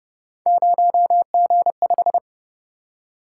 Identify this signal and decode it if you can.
Morse code, heard as a keyed tone, about 31 wpm, 700 Hz. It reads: 0G5